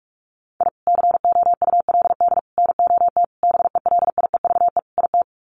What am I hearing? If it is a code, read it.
Morse code, heard as a keyed tone, about 44 wpm, 700 Hz. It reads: I COULD NOT BELIEVE IT
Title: Sets of 5 Words Encore 45wpm